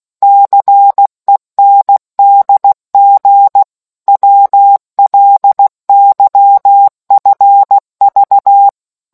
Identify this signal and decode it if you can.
Morse code, heard as a keyed tone, about 16 wpm, 800 Hz. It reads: CENDG WLYFV